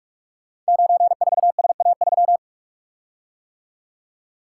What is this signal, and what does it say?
Morse code, heard as a keyed tone, about 44 wpm, 690 Hz. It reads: 94SA3
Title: Code Groups - 5 Letter:Numbers - 45wpm